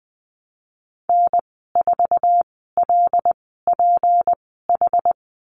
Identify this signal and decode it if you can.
Morse code, heard as a keyed tone, about 20 wpm, spaced as for 15 wpm, 700 Hz. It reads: N4LPH
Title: 179 Lesson 29g - Slash - Call Signs 20wpm